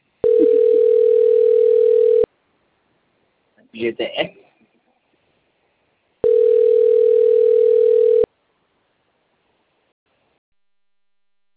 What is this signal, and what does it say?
Ringback tone (North America), 440 + 480 Hz, with speech.
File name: out- - - - - .17997